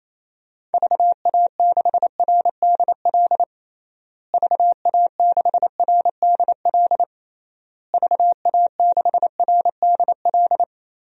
Morse code, heard as a keyed tone, about 28 wpm, 700 Hz. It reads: VA6RDL VA6RDL VA6RDL
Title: DXCC-100 Call Signs - 6 Characters - Repeated 3x - 28wpm